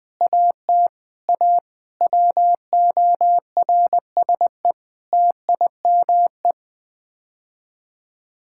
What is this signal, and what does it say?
Morse code, heard as a keyed tone, about 20 wpm, 700 Hz. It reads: AT A WORSE TIME
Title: Sets of 8 Words - 20wpm